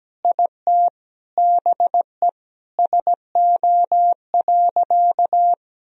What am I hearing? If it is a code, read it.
Morse code, heard as a keyed tone, about 17 wpm, 700 Hz. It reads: IT BE SO.
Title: Sentences from Top 1000 Words - Review Part 1 - 17wpm